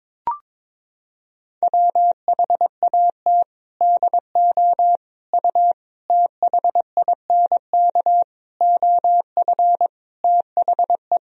Morse code, heard as a keyed tone, about 22 wpm, 700 Hz. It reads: WHAT DO U THINK OF THE
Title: Sentences from Top 300 Words and Abbreviations 22wpm